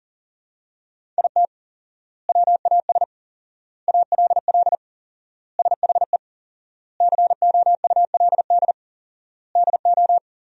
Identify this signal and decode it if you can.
Morse code, heard as a keyed tone, about 40 wpm, 700 Hz. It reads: IT WAS ALL SHE COULD DO
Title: 4x Word-Spacing - Sentences from Top 400 Words - 40wpm